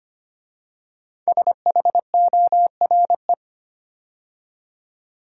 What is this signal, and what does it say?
Morse code, heard as a keyed tone, about 25 wpm, 700 Hz. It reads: SHORE
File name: 5 Letter Words - First 3 Letters Given - 25wpm